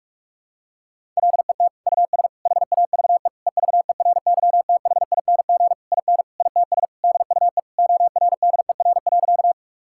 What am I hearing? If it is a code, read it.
Morse code, heard as a keyed tone, about 46 wpm, 700 Hz. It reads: LET US HAVE EVERYTHING IN ITS DUE ORDER.
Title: Sentences from Top 1000 Words - Review Part 6 - 45wpm